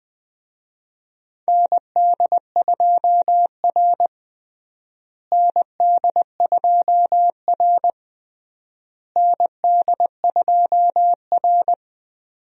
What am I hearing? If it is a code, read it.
Morse code, heard as a keyed tone, about 20 wpm, 700 Hz. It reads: ND2R ND2R ND2R
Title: USA Call Signs - 4 Characters - Repeated 3x - 20wpm